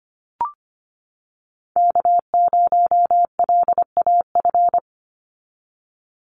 Morse code, heard as a keyed tone, about 25 wpm, 700 Hz. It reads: K0LAF
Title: NA SOTA Chaser Callsigns 25wpm